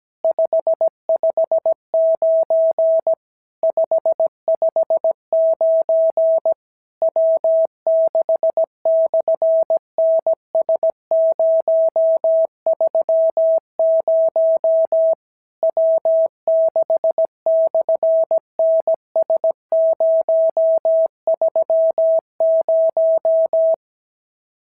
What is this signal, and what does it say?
Morse code, heard as a keyed tone, about 17 wpm, 650 Hz. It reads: 559 559 W6/NS030 W6/NS030
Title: SOTA QSO Practice 17wpm